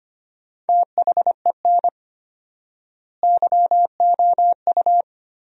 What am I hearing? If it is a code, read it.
Morse code, heard as a keyed tone, about 25 wpm, 700 Hz. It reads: THEN YOU
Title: 4x Word-Spacing - Sentences from Top 400 Words - 25wpm